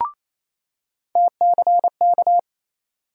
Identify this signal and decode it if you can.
Morse code, heard as a keyed tone, about 28 wpm, 700 Hz. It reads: TCK
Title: USA Call Sign Postfixes - 3 Letters 28wpm